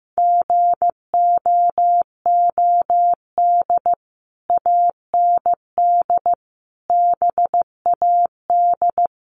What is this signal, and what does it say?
Morse code, heard as a keyed tone, about 15 wpm, 700 Hz. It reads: GOOD AND BAD
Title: Sentences from Top 1500 Words 15wpm